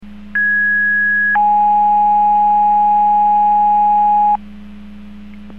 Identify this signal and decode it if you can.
Two-tone page, heard as a one-second tone, then a three-second tone, about 1650 then 830 Hz. It is heard from 0.4 to 4.4 s.